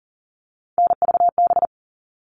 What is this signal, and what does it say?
Morse code, heard as a keyed tone, about 40 wpm, 700 Hz. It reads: NVB
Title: All Cut Numbers - 3-Digit 40wpm